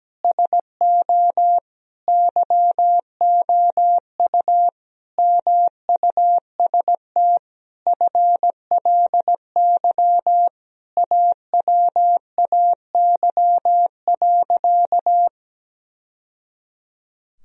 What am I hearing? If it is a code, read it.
Morse code, heard as a keyed tone, about 17 wpm, 700 Hz. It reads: SO YOU MUST FLY AWAY.
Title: Sentences from Top 1000 Words - Review Part 2 - 17wpm